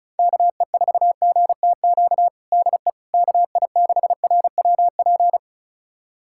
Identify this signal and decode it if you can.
Morse code, heard as a keyed tone, about 34 wpm, 700 Hz. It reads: KE4GTQ DE KI6RWP
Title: USA Callsign DE Callsign - 6 Characters - 35wpm